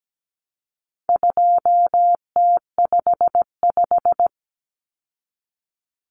Morse code, heard as a keyed tone, about 17 wpm, 700 Hz. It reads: 2T55
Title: Common Cut Numbers - 4-Digit 17wpm